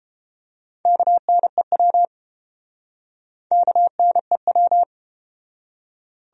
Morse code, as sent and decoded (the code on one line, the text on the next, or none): -.- -. .-- / -.- -. .--
KNEW KNEW